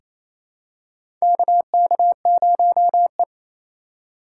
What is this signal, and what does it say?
Morse code, heard as a keyed tone, about 28 wpm, 700 Hz. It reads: KK0E
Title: USA Call Signs - 4 Characters - 3x Character Spacing - 28wpm